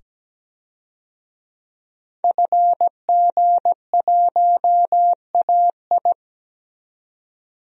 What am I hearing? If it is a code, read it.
Morse code, heard as a keyed tone, about 17 wpm, 700 Hz. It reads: FG1AI
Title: DXCC-100 Call Signs - 5 Characters - 17wpm